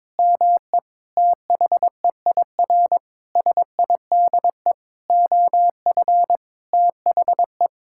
Morse code, heard as a keyed tone, about 22 wpm, 700 Hz. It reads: ME THEIR SIDE OF THE
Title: Sentences from Top 1000 Words - Review Part 5 - 22wpm